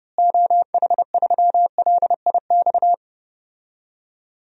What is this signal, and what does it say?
Morse code, heard as a keyed tone, about 32 wpm, 700 Hz. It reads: OH3LIX
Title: DXCC-100 Call Signs - 6 Characters - Repeated 2x - 30wpm